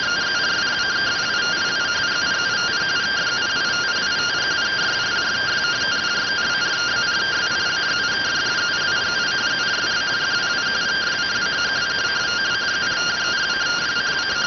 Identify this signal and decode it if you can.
Two-tone teleprinter signal, about 170 Hz shift.